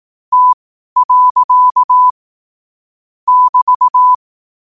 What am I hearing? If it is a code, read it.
Morse code, heard as a keyed tone, about 18 wpm, spaced as for 12 wpm, 1000 Hz. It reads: T. <BT>
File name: qotdfiles / 12wpm